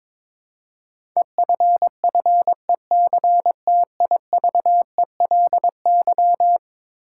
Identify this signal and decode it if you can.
Morse code, heard as a keyed tone, about 22 wpm, 700 Hz. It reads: EFFECTIVELY
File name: practice / 22wpm / Eleven Letter Words - 22wpm